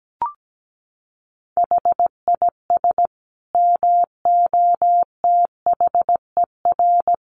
Morse code, heard as a keyed tone, about 17 wpm, 700 Hz. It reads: HIS MOTHER
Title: Sentences from Top 1200 Words 17wpm